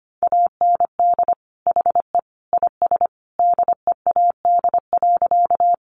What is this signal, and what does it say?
Morse code, heard as a keyed tone, about 25 wpm, 700 Hz. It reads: AND HE IS DEAD.
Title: Sentences from Top 1500 Words 25wpm